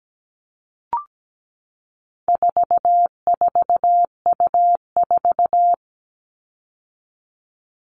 Morse code, heard as a keyed tone, about 17 wpm, 700 Hz. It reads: 44U4
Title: All Cut Numbers - 4-Digit 17wpm